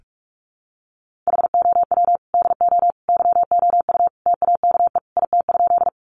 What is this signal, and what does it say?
Morse code, heard as a keyed tone, about 44 wpm, 700 Hz. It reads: HOW DO YOU TAKE IT?
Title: Sentences from Top 200 Words 45wpm